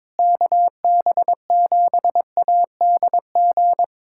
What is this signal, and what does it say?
Morse code, heard as a keyed tone, about 22 wpm, 700 Hz. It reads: KB7ADG
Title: USA Callsign DE Callsign - 6 Characters - 22wpm